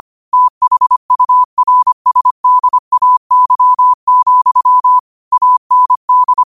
Morse code, heard as a keyed tone, about 25 wpm, 1000 Hz. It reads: THURSDAY, AND
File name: qotdfiles / 25wpm